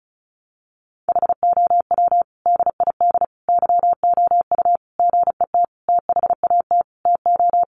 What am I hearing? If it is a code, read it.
Morse code, heard as a keyed tone, about 34 wpm, 700 Hz. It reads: HOW DID YOU GET THAT TO
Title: Sentences from Top 900 Words 35wpm